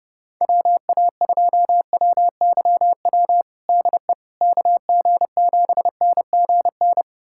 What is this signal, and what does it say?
Morse code, heard as a keyed tone, about 30 wpm, 700 Hz. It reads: WA2WYW DE KG7NGN
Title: USA Callsign DE Callsign - 6 Characters - 30wpm